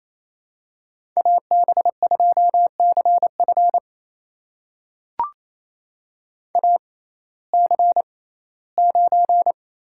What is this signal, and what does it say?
Morse code, heard as a keyed tone, about 28 wpm, spaced as for 17 wpm, 700 Hz. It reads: AB2CF AC9
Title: USA Call Signs - 5 Characters - 6x Character Spacing - 28wpm